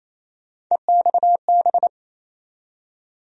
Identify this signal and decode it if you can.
Morse code, heard as a keyed tone, about 28 wpm, 700 Hz. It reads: EXB
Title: USA Call Sign Postfixes - 3 Letters 28wpm